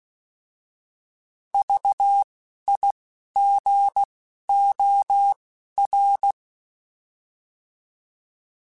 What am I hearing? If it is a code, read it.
Morse code, heard as a keyed tone, about 16 wpm, spaced as for 11 wpm, 800 Hz. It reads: VIGOR
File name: Alphabetical / Alphabetical mp3s / Vs @10